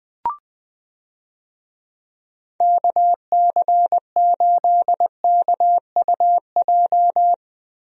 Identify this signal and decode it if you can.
Morse code, heard as a keyed tone, about 20 wpm, 700 Hz. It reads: KC8KUJ
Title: USA Call Signs - 6 Characters - Repeated 3x - 20wpm